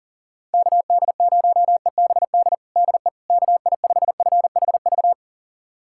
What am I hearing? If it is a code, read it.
Morse code, heard as a keyed tone, about 40 wpm, 690 Hz. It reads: KD0EBD DE KI5FHV
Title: USA Callsign DE Callsign - 6 Characters - 40wpm